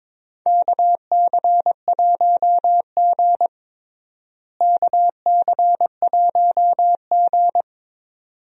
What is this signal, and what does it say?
Morse code, heard as a keyed tone, about 22 wpm, 700 Hz. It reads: KC1G KC1G
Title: USA Call Signs - 4 Characters - Repeated 2x - 22wpm